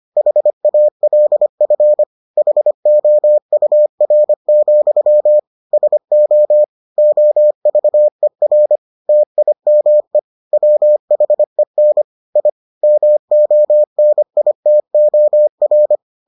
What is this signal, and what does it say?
Morse code, heard as a keyed tone, about 25 wpm, 600 Hz. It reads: HALF HOUR, SO OVER TIME WHEN I MONITOR